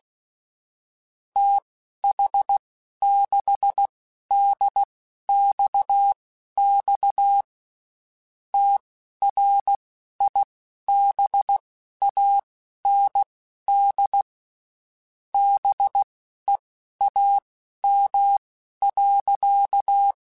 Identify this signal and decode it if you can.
Morse code, heard as a keyed tone, about 16 wpm, spaced as for 12 wpm, 800 Hz. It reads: TH6DXX TRIBAND BEAM.
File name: QSOs / QSOs-mp3s / WD0BGP 1@15